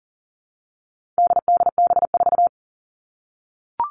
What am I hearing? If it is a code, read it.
Morse code, heard as a keyed tone, about 40 wpm, 700 Hz. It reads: DDB4
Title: All Cut Numbers - 4-Digit 40wpm